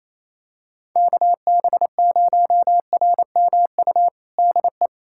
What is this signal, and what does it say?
Morse code, heard as a keyed tone, about 27 wpm, 700 Hz. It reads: KB0RMU DE
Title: USA Callsign DE Callsign - 6 Characters - 28wpm